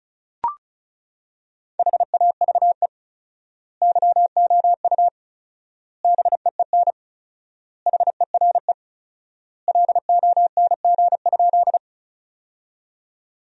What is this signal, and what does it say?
Morse code, heard as a keyed tone, about 36 wpm, 700 Hz. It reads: HAVE YOU BEEN HERE LONG?
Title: 4x Word-Spacing - Sentences from Top 200 Words - 35wpm